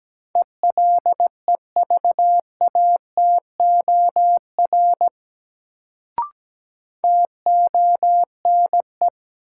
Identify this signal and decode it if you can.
Morse code, heard as a keyed tone, about 17 wpm, 700 Hz. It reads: ELEVATOR TONE
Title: Sentences from Top 1600 Words - New Word Drill - 17wpm